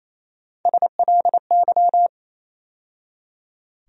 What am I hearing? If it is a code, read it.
Morse code, heard as a keyed tone, about 28 wpm, 700 Hz. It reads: SLY
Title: USA Call Sign Postfixes - 3 Letters 28wpm